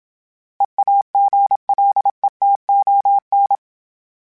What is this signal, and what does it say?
Morse code, heard as a keyed tone, about 27 wpm, 800 Hz. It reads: EAGLETON
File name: fromSMs / FromSMs mp3s / Cities@25